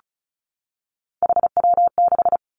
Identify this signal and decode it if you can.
Morse code, heard as a keyed tone, about 36 wpm, 700 Hz. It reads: HW6
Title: Three Challenging Letter-Number Combinations 35wpm